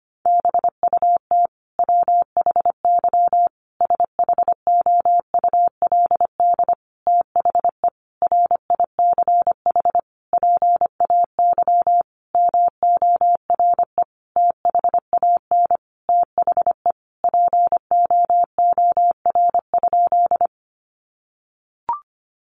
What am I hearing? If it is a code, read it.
Morse code, heard as a keyed tone, about 25 wpm, 700 Hz. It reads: BUT WHY SHOULD THE RICH PAY MORE THAN THE POOR?
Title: Sentences from Top 1100 Words 25wpm